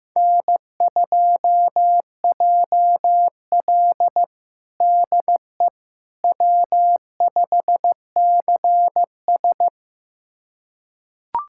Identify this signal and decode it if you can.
Morse code, heard as a keyed tone, about 15 wpm, 700 Hz. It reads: N2JL DE W5CS
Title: USA Callsign DE Callsign - 4 Characters - 15wpm